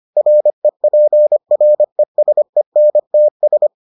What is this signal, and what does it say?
Morse code, heard as a keyed tone, about 25 wpm, 600 Hz. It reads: REPRESENTS